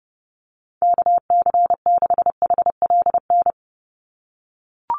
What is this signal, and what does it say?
Morse code, heard as a keyed tone, about 30 wpm, 700 Hz. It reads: KC6HLN
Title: USA Call Signs 30wpm